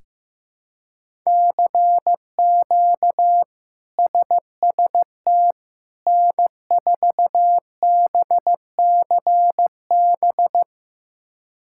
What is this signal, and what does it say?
Morse code, heard as a keyed tone, about 15 wpm, 700 Hz. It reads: CQ SST N4BCB
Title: K1USN SST QSO Practice - 15wpm